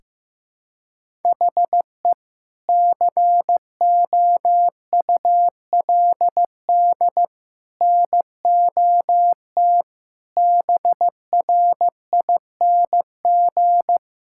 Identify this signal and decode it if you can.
Morse code, heard as a keyed tone, about 16 wpm, 700 Hz. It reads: HE COULD NOT BRING
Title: Sentences from Top 1600 Words 15wpm